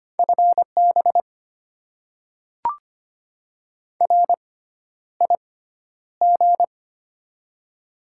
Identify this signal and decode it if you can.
Morse code, heard as a keyed tone, about 25 wpm, spaced as for 11 wpm, 700 Hz. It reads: FB RIG
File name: Top 20 QSO Elements - 6x Character Spacing - 25wpm